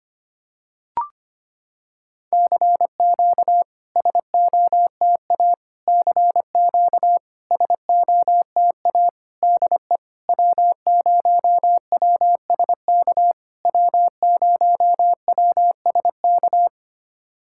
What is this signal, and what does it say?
Morse code, heard as a keyed tone, about 25 wpm, 700 Hz. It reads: CQ SOTA CQ SOTA DE W0WSK W0WSK
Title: SOTA QSO Practice 25wpm